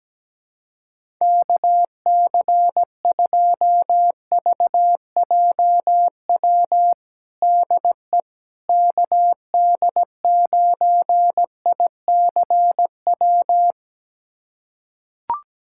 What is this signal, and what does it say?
Morse code, heard as a keyed tone, about 17 wpm, 700 Hz. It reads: KC2VJW DE KD9ICW
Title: USA Callsign DE Callsign 17wpm